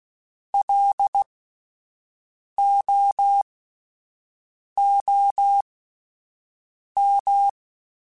Morse code, heard as a keyed tone, about 16 wpm, spaced as for 5 wpm, 800 Hz. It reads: LOOM
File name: Alphabetical / Alphabetical mp3s / Ls @5